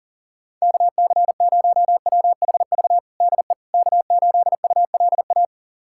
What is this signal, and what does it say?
Morse code, heard as a keyed tone, about 40 wpm, 700 Hz. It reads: KC0WHV DE K8ULA
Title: USA Callsign DE Callsign 40wpm